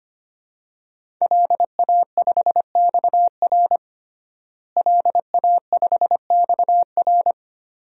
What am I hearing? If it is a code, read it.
Morse code, heard as a keyed tone, about 25 wpm, 700 Hz. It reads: LA5XR LA5XR
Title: DXCC-100 Call Signs - 5 Characters - Repeated 3x - 25wpm